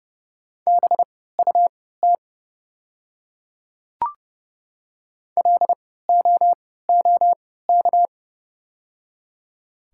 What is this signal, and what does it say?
Morse code, heard as a keyed tone, about 30 wpm, spaced as for 17 wpm, 700 Hz. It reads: BUT LOOK
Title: 3x Character Spacing - Top 200 Words - 30wpm